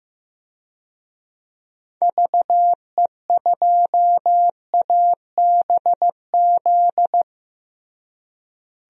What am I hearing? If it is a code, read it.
Morse code, heard as a keyed tone, about 15 wpm, 700 Hz. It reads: VE2ABZ
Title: DXCC-100 Call Signs - 6 Characters - Repeated 2x - 15wpm